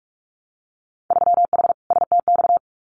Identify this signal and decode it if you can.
Morse code, heard as a keyed tone, about 46 wpm, 700 Hz. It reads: 3H STX
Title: Winter Field Day Exchanges Encore 45wpm